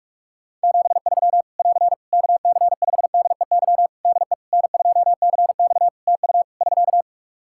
Morse code, heard as a keyed tone, about 46 wpm, 700 Hz. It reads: 73 <AR> KC5DEY DE N2CX TU <SK>
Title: Practice QSO - Exchange 4 - Ending - 45wpm